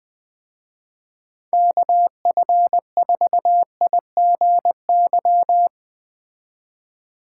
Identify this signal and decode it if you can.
Morse code, heard as a keyed tone, about 20 wpm, 700 Hz. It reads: KF4IGY